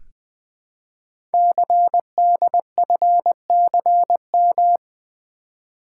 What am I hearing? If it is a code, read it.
Morse code, heard as a keyed tone, about 20 wpm, 700 Hz. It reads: CDFCM
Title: Code Groups - 5 Letters - 20wpm